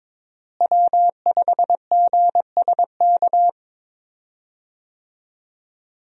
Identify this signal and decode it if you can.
Morse code, heard as a keyed tone, about 22 wpm, 700 Hz. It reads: W5GSK
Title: USA Call Signs - 5 Characters - Repeated 2x - 22wpm